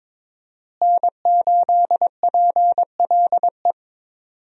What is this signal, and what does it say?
Morse code, heard as a keyed tone, about 21 wpm, 700 Hz. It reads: N8PLE
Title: USA Call Signs - 5 Characters - Repeated 3x - 22wpm